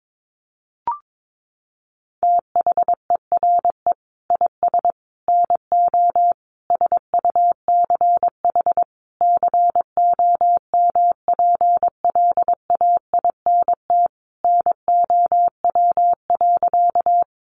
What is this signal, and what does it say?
Morse code, heard as a keyed tone, about 22 wpm, 700 Hz. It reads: THERE IS NO SUCH COMPLAINT NOW.
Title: Sentences from Top 1400 Words 22wpm